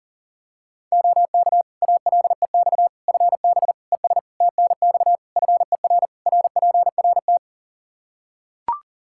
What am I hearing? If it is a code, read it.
Morse code, heard as a keyed tone, about 40 wpm, 700 Hz. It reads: OK ALEX FB ES TNX FER RPRT
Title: Practice QSO - Exchange 2 - Info - 40wpm